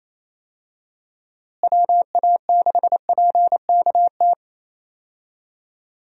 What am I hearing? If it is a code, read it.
Morse code, heard as a keyed tone, about 28 wpm, 700 Hz. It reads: WA6PKT
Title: USA Call Signs - 6 Characters - Repeated 2x - 28wpm